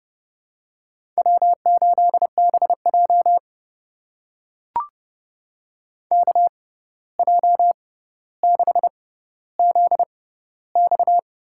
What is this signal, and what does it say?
Morse code, heard as a keyed tone, about 30 wpm, spaced as for 14 wpm, 700 Hz. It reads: W8BJ KJ6ZX
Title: USA Call Signs - 6x Character Spacing - 30wpm